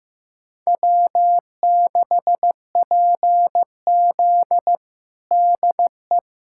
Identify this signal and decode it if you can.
Morse code, heard as a keyed tone, about 15 wpm, 700 Hz. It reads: W6PZ DE
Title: USA Callsign DE Callsign 15wpm